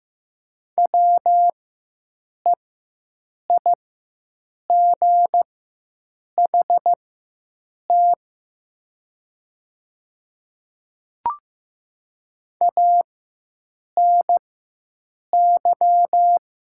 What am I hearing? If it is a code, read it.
Morse code, heard as a keyed tone, about 15 wpm, spaced as for 7 wpm, 700 Hz. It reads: WEIGHT ANY
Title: 4x Character Spacing - Top 500 Words - 15wpm